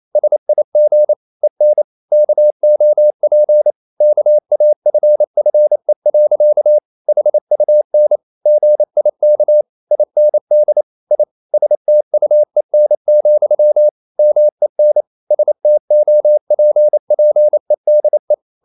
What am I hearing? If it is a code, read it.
Morse code, heard as a keyed tone, about 28 wpm, 600 Hz. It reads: SIG EN KOP KAFFE. HUN GIK IND I STUEN, MEN STOPPEDE
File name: Ferie QRQ 28wpm